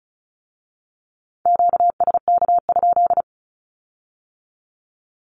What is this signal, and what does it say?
Morse code, heard as a keyed tone, about 34 wpm, 700 Hz. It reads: QSK?
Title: Q-Codes with Spoken Meaning - 35wpm